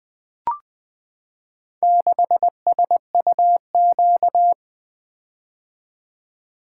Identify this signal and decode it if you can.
Morse code, heard as a keyed tone, about 20 wpm, 700 Hz. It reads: 6SUQ
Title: Four Letter-Number Combinations - 20wpm